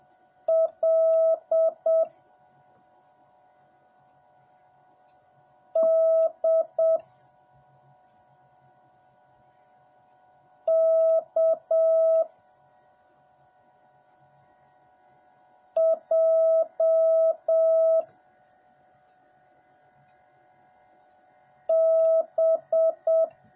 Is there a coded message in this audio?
Yes: Morse code from 0.5 to 23.2 s, heard as a keyed tone, about 7 wpm, 650 Hz. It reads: L D K J B